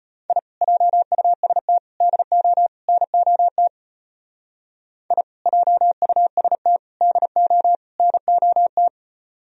Morse code, heard as a keyed tone, about 31 wpm, 700 Hz. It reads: I JUST DO NOT I JUST DO NOT
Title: Speed-Racing - Sets of 4 Words 25wpm